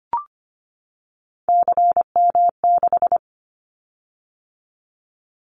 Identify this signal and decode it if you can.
Morse code, heard as a keyed tone, about 25 wpm, 700 Hz. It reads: CM6
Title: Callsign Prefixes for Top 100 DXCC - 25wpm